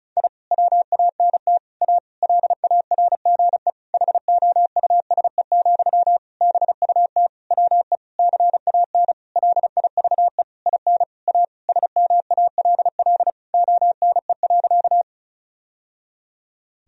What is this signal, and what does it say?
Morse code, heard as a keyed tone, about 34 wpm, 700 Hz. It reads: I WANT A LARGE HOUSE, BUT WE CAN LIVE IN A SMALL ONE.
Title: Sentences from Top 1000 Words - Review Part 1 - 35wpm